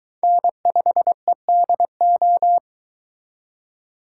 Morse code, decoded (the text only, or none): N5EDO